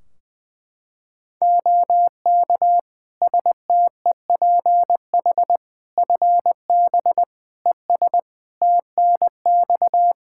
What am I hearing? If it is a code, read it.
Morse code, heard as a keyed tone, about 20 wpm, 700 Hz. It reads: OK STEPH FB ES TNX
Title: Practice QSO - Exchange 2 - Info - 20wpm